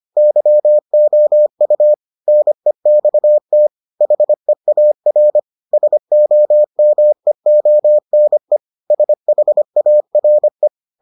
Morse code, heard as a keyed tone, about 25 wpm, 600 Hz. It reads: YOU NEXT HEAR SOMEONE SHARE